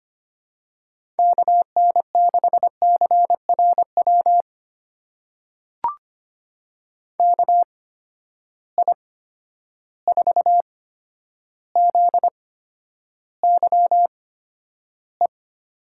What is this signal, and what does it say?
Morse code, heard as a keyed tone, about 25 wpm, spaced as for 11 wpm, 700 Hz. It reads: KN6CRW KI4ZYE
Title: USA Call Signs - 6 Characters - 8x Character Spacing - 25wpm